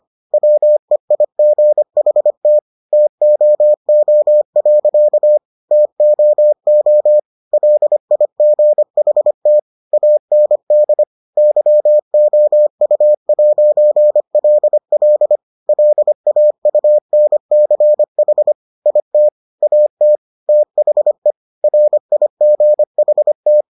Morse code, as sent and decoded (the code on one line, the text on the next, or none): .-- . .. --. .... - / - --- --- .-.-.- / - --- --- / .-.. .. --. .... - / .- -. -.. / -.-- --- ..- .----. .-.. .-.. / .-.. .- ..- -. -.-. .... / .. - / .- - / - .... / .-. .. --. .... -
WEIGHT TOO. TOO LIGHT AND YOU'LL LAUNCH IT AT THE RIGHT